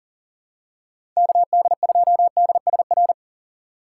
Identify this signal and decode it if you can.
Morse code, heard as a keyed tone, about 40 wpm, 700 Hz. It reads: KD2DSR
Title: USA Call Signs - 6 Characters - 4x Character Spacing - 40wpm